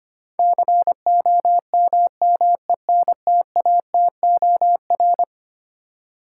Morse code, heard as a keyed tone, about 25 wpm, 700 Hz. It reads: COMMENTATOR
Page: morse-code-ninja-practice-25wpm